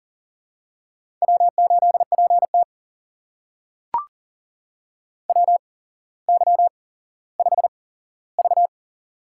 Morse code, heard as a keyed tone, about 40 wpm, spaced as for 17 wpm, 700 Hz. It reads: W8PT WY5V
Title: USA Call Signs - 4 Characters - 8x Character Spacing - 40wpm